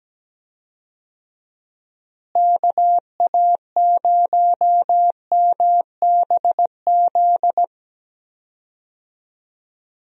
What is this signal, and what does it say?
Morse code, heard as a keyed tone, about 17 wpm, 700 Hz. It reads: KA0MBZ
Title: USA Call Signs - 6 Characters - Repeated 2x - 17wpm